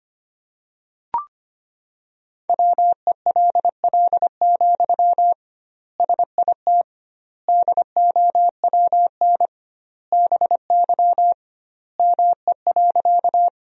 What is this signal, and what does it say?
Morse code, heard as a keyed tone, about 25 wpm, 700 Hz. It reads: WELL, SIT DOWN BY ME.
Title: 2x Word-Spacing - Sentences from Top 500 Words - 25wpm